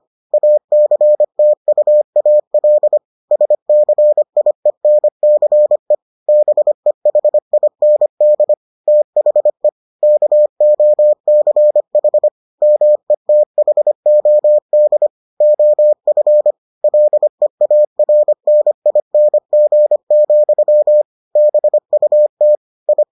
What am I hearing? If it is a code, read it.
Morse code, heard as a keyed tone, about 25 wpm, 600 Hz. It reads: ACTUAL SCIENCE BEHIND THE KOCH METHOD OF LEARNING, BUT I